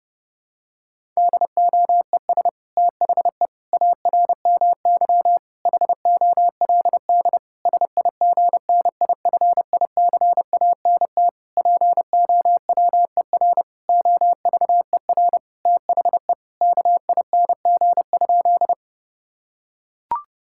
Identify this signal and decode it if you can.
Morse code, heard as a keyed tone, about 30 wpm, 700 Hz. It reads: DOES THE ARMY HOLD SIGNIFICANT POWER OVER THE KING?
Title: Sentences from Top 1000 Words - Review Part 6 - 30wpm